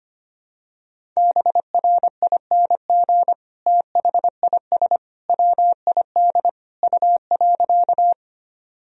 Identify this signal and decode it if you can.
Morse code, heard as a keyed tone, about 25 wpm, 700 Hz. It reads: BRING THIS WID U.
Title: Sentences from Top 500 Words and Abbreviations 25wpm